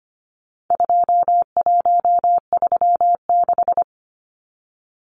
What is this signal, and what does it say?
Morse code, heard as a keyed tone, about 25 wpm, 700 Hz. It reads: 2136
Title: Numbers - 4-Digit 25wpm